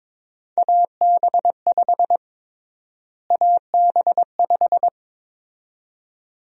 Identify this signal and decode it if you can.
Morse code, heard as a keyed tone, about 22 wpm, 700 Hz. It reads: AB5 AB5
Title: USA Call Sign Prefixes - Repeated 2x - 22wpm